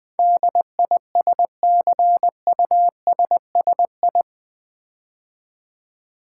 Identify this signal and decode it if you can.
Morse code, heard as a keyed tone, about 20 wpm, 700 Hz. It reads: DISCUSSI
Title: Build A Word - 4 Letters At A Time - 10-Letter Words - 20wpm